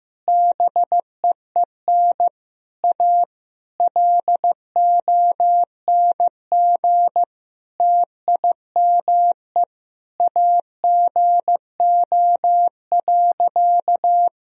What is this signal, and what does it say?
Morse code, heard as a keyed tone, about 16 wpm, 700 Hz. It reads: BEEN A LONG TIME AGO.
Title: Sentences from Top 1000 Words - Review Part 3 - 15wpm